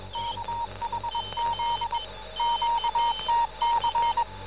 Morse code, heard as a keyed tone, about 22 wpm, 930 Hz, with noise. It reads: KHZ QTC